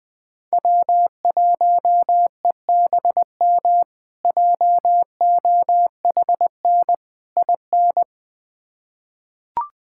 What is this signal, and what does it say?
Morse code, heard as a keyed tone, about 20 wpm, 700 Hz. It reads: W1EBM JOHN IN
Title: K1USN SST QSO Practice - 20wpm